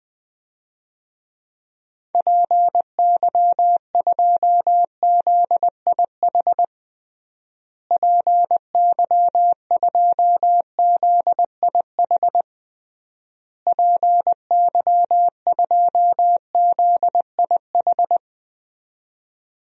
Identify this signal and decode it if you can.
Morse code, heard as a keyed tone, about 20 wpm, 700 Hz. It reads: PY2ZIH PY2ZIH PY2ZIH
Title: DXCC-100 Call Signs - 6 Characters - Repeated 3x - 20wpm